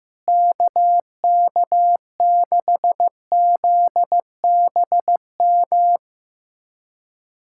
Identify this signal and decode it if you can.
Morse code, heard as a keyed tone, about 15 wpm, 700 Hz. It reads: KK6ZBM
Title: USA Callsign DE Callsign 15wpm